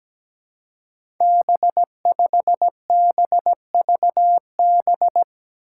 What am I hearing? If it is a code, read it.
Morse code, heard as a keyed tone, about 17 wpm, 700 Hz. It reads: B5BVB
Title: Five Challenging Letter-Number Combinations 17wpm